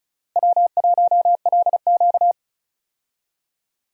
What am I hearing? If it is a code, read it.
Morse code, heard as a keyed tone, about 36 wpm, 700 Hz. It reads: W1LQ
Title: USA Call Signs - 4 Characters - Repeated 3x - 35wpm